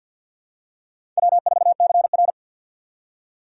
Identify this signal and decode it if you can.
Morse code, heard as a keyed tone, about 47 wpm, 700 Hz. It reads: W4CR
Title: USA Call Signs - 4 Characters - 6x Character Spacing - 50wpm